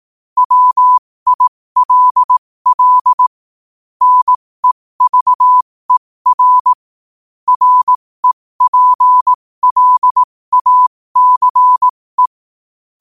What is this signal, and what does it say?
Morse code, heard as a keyed tone, about 18 wpm, spaced as for 15 wpm, 1000 Hz. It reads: WILL NEVER REPLACE
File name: qotdfiles / 15wpm